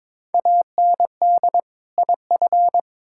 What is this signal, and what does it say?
Morse code, heard as a keyed tone, about 22 wpm, 700 Hz. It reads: AND IF
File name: Build A Phrase - 2-Word Phrases - 22wpm